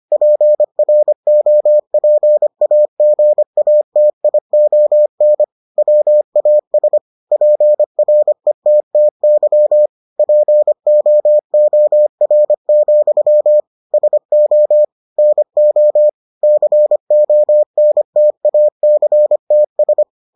Morse code, heard as a keyed tone, about 25 wpm, 600 Hz. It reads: PROPAGATION WAS PRETTY POOR, SO NO CONTACTS